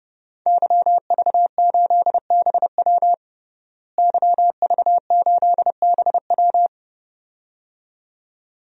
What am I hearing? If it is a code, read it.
Morse code, heard as a keyed tone, about 32 wpm, 700 Hz. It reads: YV8BW YV8BW
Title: DXCC-100 Call Signs - 5 Characters - Repeated 2x - 30wpm